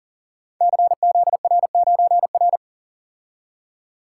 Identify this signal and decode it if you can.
Morse code, heard as a keyed tone, about 38 wpm, 700 Hz. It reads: CZR9R
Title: Code Groups - 5 Letter:Numbers - 40wpm